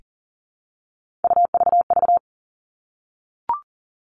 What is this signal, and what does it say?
Morse code, heard as a keyed tone, about 40 wpm, 700 Hz. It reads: UVV
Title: All Cut Numbers - 3-Digit 40wpm